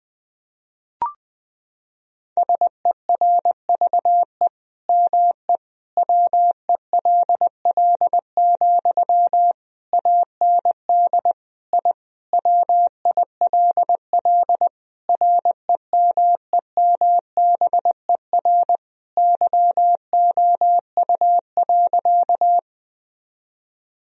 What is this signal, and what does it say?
Morse code, heard as a keyed tone, about 20 wpm, 700 Hz. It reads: SERVE ME WELL, AND I WILL REMEMBER YOU.
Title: Sentences from Top 1000 Words - Review Part 2 - 20wpm